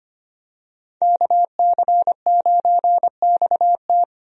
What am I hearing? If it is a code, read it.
Morse code, heard as a keyed tone, about 25 wpm, 700 Hz. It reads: KC9XT
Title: USA Call Signs - 5 Characters - Repeated 2x - 25wpm